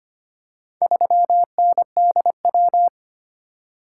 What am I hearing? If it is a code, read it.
Morse code, heard as a keyed tone, about 25 wpm, 700 Hz. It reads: 3NDW
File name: practice / 25wpm / Four Letter-Number Combinations - 25wpm